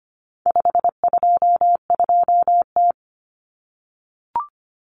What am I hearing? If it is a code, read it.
Morse code, heard as a keyed tone, about 25 wpm, 700 Hz. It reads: 522T
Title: Common Cut Numbers - 4-Digit 25wpm